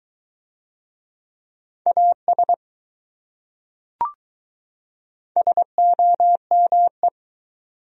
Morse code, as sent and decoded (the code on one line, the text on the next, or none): .- ... / ... --- -- .
AS SOME